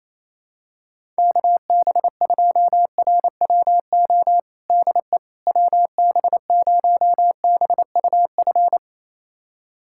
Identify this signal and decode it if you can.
Morse code, heard as a keyed tone, about 27 wpm, 700 Hz. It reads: KB2RWO DE WB0BUF
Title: USA Callsign DE Callsign - 6 Characters - 28wpm